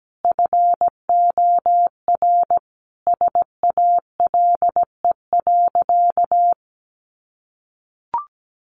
Morse code, heard as a keyed tone, about 17 wpm, 700 Hz. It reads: FOR SALE.
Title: Sentences from Top 1400 Words 17wpm